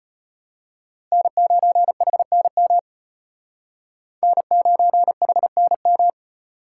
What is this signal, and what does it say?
Morse code, heard as a keyed tone, about 32 wpm, 700 Hz. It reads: N9HNM N9HNM